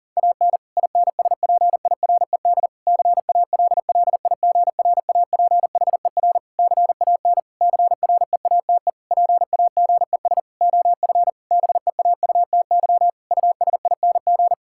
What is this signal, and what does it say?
Morse code, heard as a keyed tone, about 40 wpm, 700 Hz. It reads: AN INSPIRED CALLIGRAPHER CAN CREATE PAGES OF BEAUTY USING
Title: Pangram Challenge v2 - 40wpm